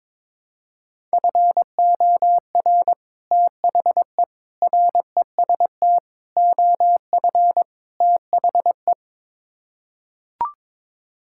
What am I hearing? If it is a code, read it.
Morse code, heard as a keyed tone, about 21 wpm, 700 Hz. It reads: FOR THE REST OF THE
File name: practice / 22wpm / Build A Phrase - 5-Word Phrases - 22wpm